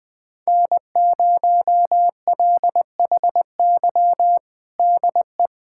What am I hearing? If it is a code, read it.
Morse code, heard as a keyed tone, about 20 wpm, 700 Hz. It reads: N0LHY DE
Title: USA Callsign DE Callsign - 5 Characters - 20wpm